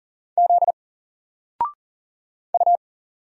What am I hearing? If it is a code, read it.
Morse code, heard as a keyed tone, about 38 wpm, 700 Hz. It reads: Z U
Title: Single Character (Letter:Number:Punctuation:Prosign) 40wpm